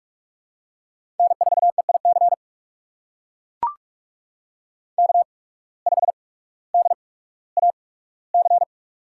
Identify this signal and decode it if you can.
Morse code, heard as a keyed tone, about 44 wpm, spaced as for 19 wpm, 700 Hz. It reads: N4EIC K5DAC
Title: USA Call Signs - 5 Characters - 8x Character Spacing - 45wpm